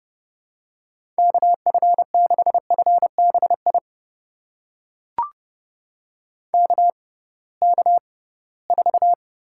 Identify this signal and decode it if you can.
Morse code, heard as a keyed tone, about 30 wpm, spaced as for 19 wpm, 700 Hz. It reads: KF6FBI KK4
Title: USA Call Signs - 6 Characters - 6x Character Spacing - 30wpm